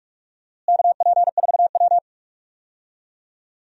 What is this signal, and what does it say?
Morse code, heard as a keyed tone, about 46 wpm, 700 Hz. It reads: KP4W
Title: DXCC-100 Call Signs - 4 Characters - Repeated 2x - 45wpm